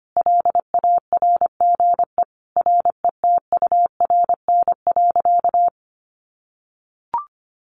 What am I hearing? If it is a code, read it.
Morse code, heard as a keyed tone, about 25 wpm, 700 Hz. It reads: LARGE RETURN.
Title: Sentences from Top 1100 Words 25wpm